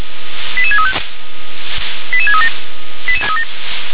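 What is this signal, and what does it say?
Five-tone selective call (ZVEI), heard as quick runs of five tones, tones of about 70 ms. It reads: 88635 88637
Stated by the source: ZVEI five-tone call 88597 received by SDR.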